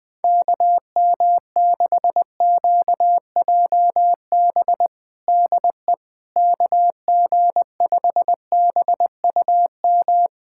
Morse code, heard as a keyed tone, about 21 wpm, 700 Hz. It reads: KM6QJB DE KG5BUM
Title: USA Callsign DE Callsign - 6 Characters - 20wpm